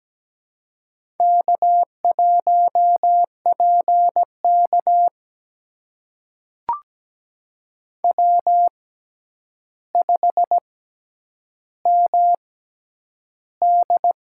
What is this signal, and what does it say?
Morse code, heard as a keyed tone, about 17 wpm, spaced as for 9 wpm, 700 Hz. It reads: K1PK W5MD
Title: USA Call Signs - 4 Characters - 6x Character Spacing - 17wpm